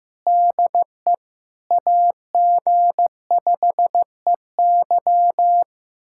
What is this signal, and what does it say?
Morse code, heard as a keyed tone, about 15 wpm, 700 Hz. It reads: DE AG5EY
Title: USA Callsign DE Callsign - 5 Characters - 15wpm